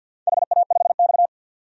Morse code, heard as a keyed tone, about 48 wpm, 700 Hz. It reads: HA5X
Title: DXCC-100 Call Signs - 4 Characters - 50wpm